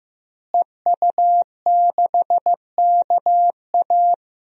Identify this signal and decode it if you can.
Morse code, heard as a keyed tone, about 15 wpm, 700 Hz. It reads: EU6KA
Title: DXCC-100 Call Signs - 5 Characters - Repeated 2x - 15wpm